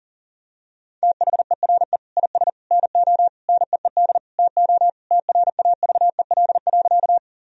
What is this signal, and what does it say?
Morse code, heard as a keyed tone, about 40 wpm, 700 Hz. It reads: THERE IS NO NEED TO TRAVEL.
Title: Sentences from Top 1000 Words - Review Part 2 - 40wpm